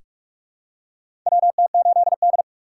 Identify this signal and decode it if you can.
Morse code, heard as a keyed tone, about 46 wpm, 690 Hz. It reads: WT8D
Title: USA Call Signs - 4 Characters - 3x Character Spacing - 45wpm